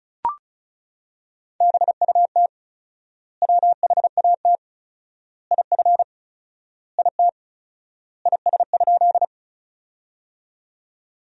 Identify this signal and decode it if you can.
Morse code, heard as a keyed tone, about 34 wpm, 700 Hz. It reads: BUT WHAT IF IT IS?
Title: 4x Word-Spacing - Sentences from Top 100 Words - 35wpm